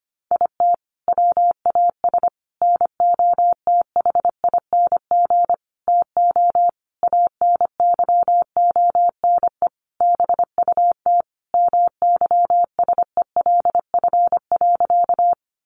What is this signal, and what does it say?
Morse code, heard as a keyed tone, about 25 wpm, 700 Hz. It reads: IT WAS NOTHING TO ANYONE BUT MYSELF.
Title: Sentences from Top 1300 Words 25wpm